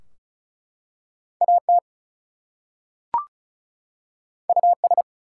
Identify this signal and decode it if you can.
Morse code, heard as a keyed tone, about 33 wpm, 700 Hz. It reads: AT US
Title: Two Letter Words - 35wpm